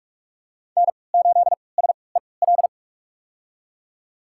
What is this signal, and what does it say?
Morse code, heard as a keyed tone, about 45 wpm, spaced as for 25 wpm, 700 Hz. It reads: N8SEL
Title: USA Call Signs - 5 Characters - 3x Character Spacing - 45wpm